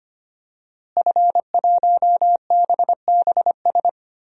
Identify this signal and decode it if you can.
Morse code, heard as a keyed tone, about 25 wpm, 700 Hz. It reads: F1BBS
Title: DXCC-100 Call Signs - 5 Characters - Repeated 3x - 25wpm